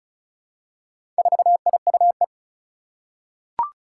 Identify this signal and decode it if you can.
Morse code, heard as a keyed tone, about 36 wpm, 700 Hz. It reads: 4IUE
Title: Four Challenging Letter-Number Combinations 35wpm